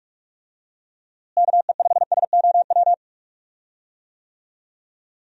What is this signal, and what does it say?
Morse code, heard as a keyed tone, about 46 wpm, 700 Hz. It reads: KE5SOW
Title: USA Call Signs - 6 Characters - Repeated 3x - 45wpm